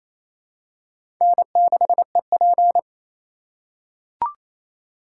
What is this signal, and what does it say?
Morse code, heard as a keyed tone, about 28 wpm, 700 Hz. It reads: N6EP
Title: USA Call Signs - 3x Character Spacing - 28wpm